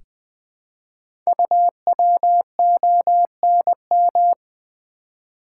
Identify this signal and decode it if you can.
Morse code, heard as a keyed tone, about 20 wpm, 700 Hz. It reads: UWONM
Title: Code Groups - 5 Letters - 20wpm